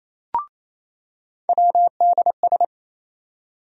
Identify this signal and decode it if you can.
Morse code, heard as a keyed tone, about 28 wpm, 700 Hz. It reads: WDS
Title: USA Call Sign Postfixes - 3 Letters 28wpm